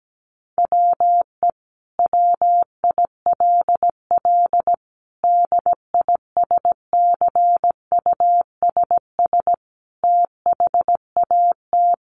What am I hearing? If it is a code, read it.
Morse code, heard as a keyed tone, about 17 wpm, 700 Hz. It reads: WE WILL DISCUSS THAT
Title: Sentences from Top 1000 Words 17wpm